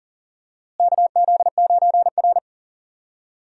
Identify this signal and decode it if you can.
Morse code, heard as a keyed tone, about 38 wpm, 700 Hz. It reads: KZ9R
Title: USA Call Signs - 4 Characters - Repeated 3x - 40wpm